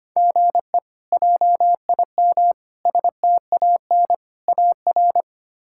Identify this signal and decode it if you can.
Morse code, heard as a keyed tone, about 25 wpm, 700 Hz. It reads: GE JIM STAN AR
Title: K1USN SST QSO Practice - 25wpm